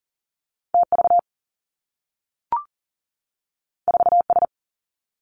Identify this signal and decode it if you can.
Morse code, heard as a keyed tone, about 42 wpm, 700 Hz. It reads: T V 4 S
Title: Two Challenging Letter-Number Combinations 40wpm